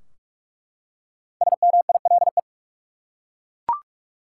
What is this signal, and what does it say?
Morse code, heard as a keyed tone, about 41 wpm, 700 Hz. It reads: SMILE
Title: Five Letter Words - 45wpm